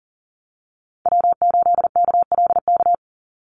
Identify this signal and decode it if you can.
Morse code, heard as a keyed tone, about 40 wpm, 700 Hz. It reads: W8KLK
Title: USA Call Signs 40wpm